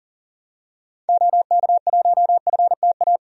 Morse code, heard as a keyed tone, about 40 wpm, 700 Hz. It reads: OK1FTA
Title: DXCC-100 Call Signs - 6 Characters - 40wpm